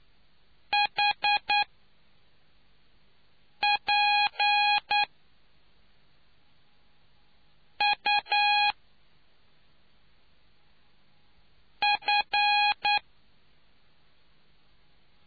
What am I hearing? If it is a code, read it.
Morse code, heard as a keyed tone, about 9 wpm, spaced as for 4 wpm, 820 Hz. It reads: HP U F